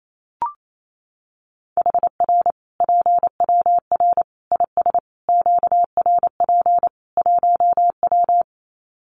Morse code, heard as a keyed tone, about 28 wpm, 700 Hz. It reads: HR PWR IS QRP 1W
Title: Common QSO Phrases 28wpm